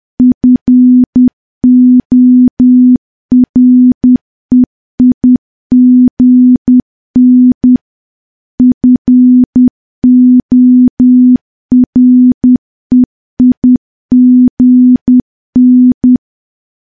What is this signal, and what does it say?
Morse code, heard as a keyed tone, about 10 wpm, 250 Hz. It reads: FOREIGN FOREIGN
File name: morse / morse-training-10WPM